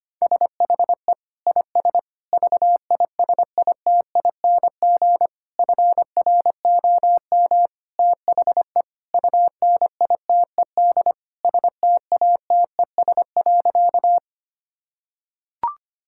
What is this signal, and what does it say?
Morse code, heard as a keyed tone, about 25 wpm, 700 Hz. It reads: SHE IS VISITING FROM THE UNITED STATES.
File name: practice / 25wpm / Sentences with 4-Word Phrases - 25wpm